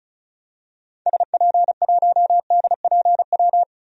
Morse code, heard as a keyed tone, about 36 wpm, 700 Hz. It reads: SP1DPW
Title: DXCC-100 Call Signs - 6 Characters - Repeated 2x - 35wpm